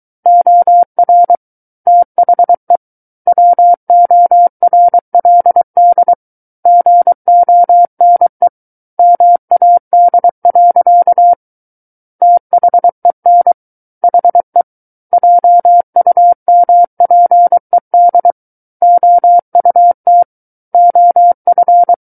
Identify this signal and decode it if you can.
Morse code, heard as a keyed tone, about 23 wpm, 700 Hz. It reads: OR THE WORLD GONE MAD. THEN HE JUMPED OUT OF
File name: Book One / 14-Chapter 14 - 23 WPM